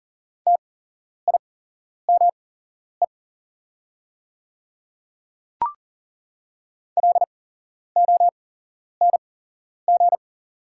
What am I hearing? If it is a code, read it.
Morse code, heard as a keyed tone, about 38 wpm, spaced as for 11 wpm, 700 Hz. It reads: TIME LONG
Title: 8x Character Spacing - Top 100 Words - 40wpm